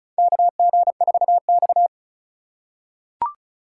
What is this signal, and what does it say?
Morse code, heard as a keyed tone, about 36 wpm, 700 Hz. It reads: KG4X
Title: USA Call Signs - 4 Characters - 2x Character Spacing - 35wpm